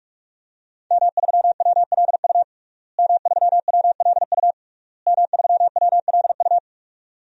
Morse code, heard as a keyed tone, about 47 wpm, 700 Hz. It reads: M3WLU M3WLU M3WLU
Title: DXCC-100 Call Signs - 5 Characters - Repeated 3x - 45wpm